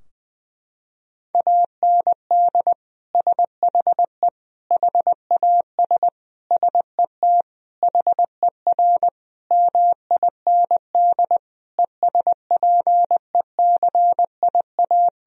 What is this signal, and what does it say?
Morse code, heard as a keyed tone, about 21 wpm, 700 Hz. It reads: AND SHE HAS SET HER MIND ESPECIA
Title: Sentences from Top 1000 Words - Review Part 4 - 20wpm